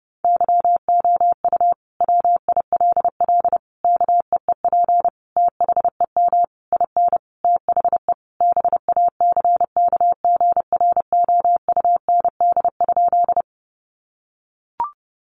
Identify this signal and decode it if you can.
Morse code, heard as a keyed tone, about 30 wpm, 700 Hz. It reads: YOU WILL KEEP THEM IN THE BACKGROUND?
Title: Sentences from Top 1300 Words 30wpm